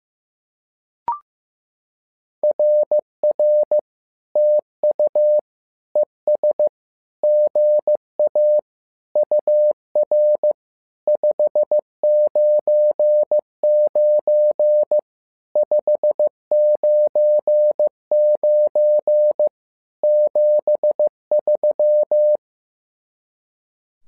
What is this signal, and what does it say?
Morse code, heard as a keyed tone, about 15 wpm, 600 Hz. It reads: RR TU ES GA UR 599 599 73
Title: SOTA QSO Practice 15wpm